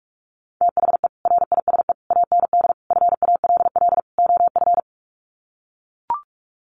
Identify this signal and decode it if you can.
Morse code, heard as a keyed tone, about 44 wpm, 690 Hz. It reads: THE RISE AND FALL OF
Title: Sets of 5 Words Encore2 45wpm